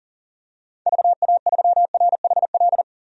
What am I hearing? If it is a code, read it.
Morse code, heard as a keyed tone, about 40 wpm, 700 Hz. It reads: VA3RHL
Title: DXCC-100 Call Signs - 6 Characters - 40wpm